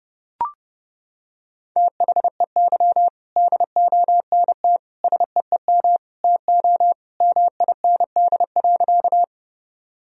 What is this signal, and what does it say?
Morse code, heard as a keyed tone, about 30 wpm, 700 Hz. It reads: THEY DONT SEEM TO MIND.
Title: Sentences from Top 1000 Words - Review Part 1 - 30wpm